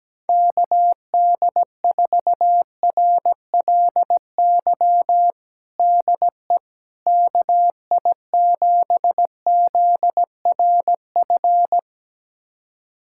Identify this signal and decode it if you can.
Morse code, heard as a keyed tone, about 17 wpm, 700 Hz. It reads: KD4RLY DE KI7ZRF
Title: USA Callsign DE Callsign 17wpm